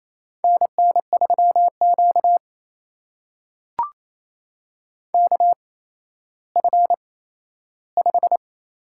Morse code, heard as a keyed tone, about 28 wpm, spaced as for 14 wpm, 700 Hz. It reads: NN3Q KF5
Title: USA Call Signs - 8x Character Spacing - 28wpm